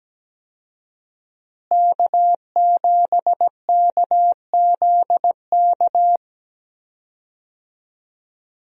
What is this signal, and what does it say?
Morse code, heard as a keyed tone, about 17 wpm, 700 Hz. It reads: K7KZK
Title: USA Call Signs - 5 Characters - Repeated 2x - 17wpm